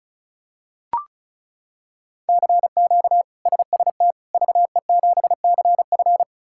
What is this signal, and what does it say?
Morse code, heard as a keyed tone, about 36 wpm, 700 Hz. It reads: CQ SST VE7CF
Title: K1USN SST QSO Practice - 35wpm